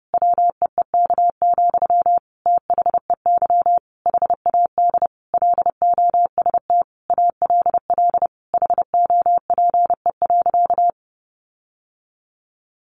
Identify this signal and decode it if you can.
Morse code, heard as a keyed tone, about 30 wpm, 700 Hz. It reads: WEEK, THEY HAD LOST ALL HOPE.
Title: Sentences from Top 1400 Words 30wpm